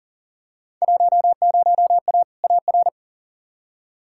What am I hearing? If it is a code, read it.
Morse code, heard as a keyed tone, about 40 wpm, 700 Hz. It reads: 10A AR
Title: Field Day Exchanges Encore - 40wpm